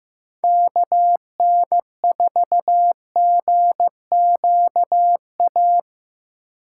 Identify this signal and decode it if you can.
Morse code, heard as a keyed tone, about 18 wpm, 700 Hz. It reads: KN4GQA
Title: Speed Racing - USA 2x3 Call Signs 15wpm